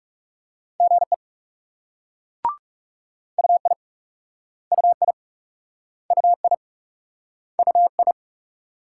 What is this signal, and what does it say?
Morse code, heard as a keyed tone, about 41 wpm, 700 Hz. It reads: GE UI UI UI UI